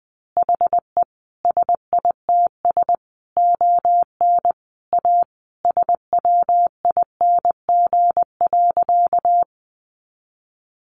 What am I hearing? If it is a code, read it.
Morse code, heard as a keyed tone, about 20 wpm, 700 Hz. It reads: HE SITS ON A SWING.
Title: Additional Easy Phrases 20wpm